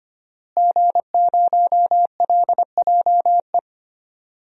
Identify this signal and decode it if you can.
Morse code, heard as a keyed tone, about 25 wpm, 700 Hz. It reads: G0LJE
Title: DXCC-100 Call Signs - 5 Characters - Repeated 2x - 25wpm